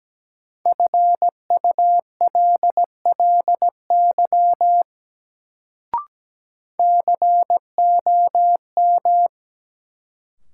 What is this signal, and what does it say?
Morse code, heard as a keyed tone, about 17 wpm, 700 Hz. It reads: FULLY COM
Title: Sentences from Top 1300 Words - New Word Drill - 17wpm